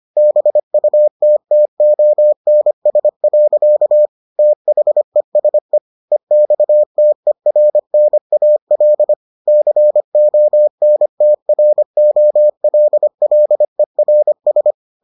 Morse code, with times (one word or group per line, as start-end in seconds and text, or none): 0.2-4.1 BUTTONS.
4.4-5.8 THESE
6.1-9.1 EXTERNAL
9.5-14.7 CONTROLLERS